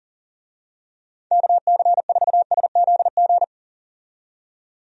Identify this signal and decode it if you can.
Morse code, heard as a keyed tone, about 40 wpm, 700 Hz. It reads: KC4SZG
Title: USA Call Signs - 6 Characters - 40wpm